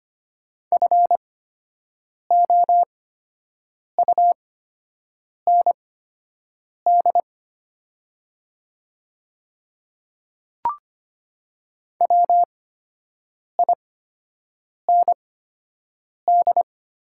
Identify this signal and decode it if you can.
Morse code, heard as a keyed tone, about 25 wpm, spaced as for 7 wpm, 700 Hz. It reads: FOUND WIND
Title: 8x Character Spacing - Top 500 Words - 25wpm